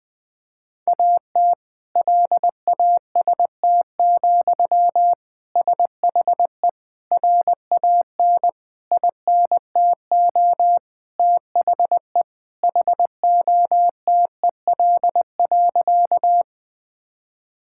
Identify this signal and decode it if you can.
Morse code, heard as a keyed tone, about 21 wpm, 700 Hz. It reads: AT LAST, SHE RAN INTO THE HOTEL.
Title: Sentences from Top 1600 Words 20wpm